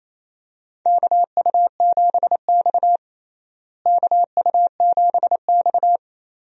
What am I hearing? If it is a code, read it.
Morse code, heard as a keyed tone, about 27 wpm, 700 Hz. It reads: KU7X KU7X
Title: USA Call Signs - 4 Characters - Repeated 2x - 28wpm